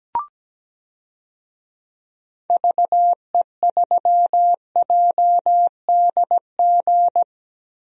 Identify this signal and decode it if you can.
Morse code, heard as a keyed tone, about 17 wpm, 700 Hz. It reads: VE3JDG
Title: DXCC-100 Call Signs - 6 Characters - Repeated 3x - 17wpm